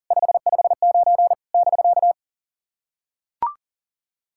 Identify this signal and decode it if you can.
Morse code, heard as a keyed tone, about 38 wpm, 690 Hz. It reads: 559 <BK>
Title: SOTA QSO Practice 40wpm